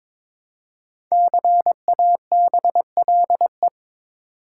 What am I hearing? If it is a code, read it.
Morse code, heard as a keyed tone, about 22 wpm, 700 Hz. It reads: CABLE
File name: 5 Letter Words - First 3 Letters Given - 22wpm